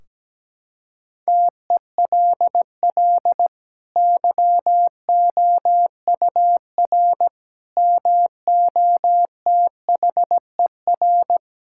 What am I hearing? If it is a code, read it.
Morse code, heard as a keyed tone, about 17 wpm, 700 Hz. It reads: TELL YOUR MOTHER
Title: Sentences from Top 1600 Words 17wpm